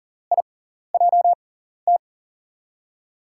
Morse code, heard as a keyed tone, about 40 wpm, spaced as for 14 wpm, 700 Hz. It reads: IJT
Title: USA Call Signs - 5 Characters - 6x Character Spacing - 40wpm